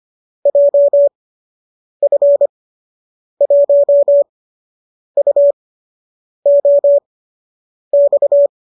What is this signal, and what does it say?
Morse code, heard as a keyed tone, about 25 wpm, spaced as for 8 wpm, 570 Hz. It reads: JF1UOX